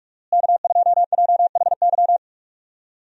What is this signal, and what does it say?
Morse code, heard as a keyed tone, about 46 wpm, 700 Hz. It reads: K2JHY
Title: USA Call Signs - 5 Characters - Repeated 2x - 45wpm